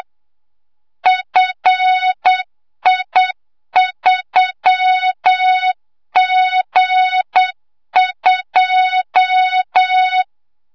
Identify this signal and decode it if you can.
Morse code, heard as a keyed tone, about 8 wpm, 3700 Hz, with noise. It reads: FI3G2
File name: Morze-HA2MN / hangp / 2 06 GYAKORLAT